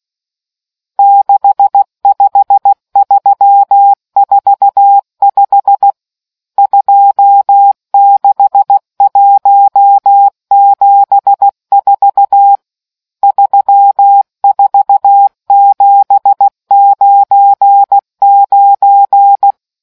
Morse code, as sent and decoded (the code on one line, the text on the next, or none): -.... ..... ...-- ....- ..... / ..--- -.... .---- --... ....- / ...-- ....- --... ----. ----.
65345 26174 34799